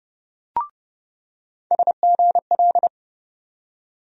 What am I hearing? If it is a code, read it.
Morse code, heard as a keyed tone, about 30 wpm, 700 Hz. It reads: SGL
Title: USA Call Sign Postfixes - 3 Letters 30wpm